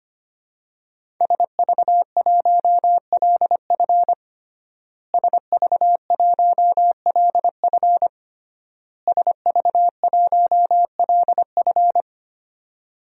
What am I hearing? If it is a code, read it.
Morse code, heard as a keyed tone, about 25 wpm, 700 Hz. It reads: SV1LF SV1LF SV1LF
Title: DXCC-100 Call Signs - 5 Characters - Repeated 3x - 25wpm